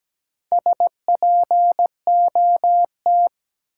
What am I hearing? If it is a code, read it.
Morse code, heard as a keyed tone, about 17 wpm, 700 Hz. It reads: SPOT
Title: Sentences from Top 1100 Words - New Word Drill Alt - 17wpm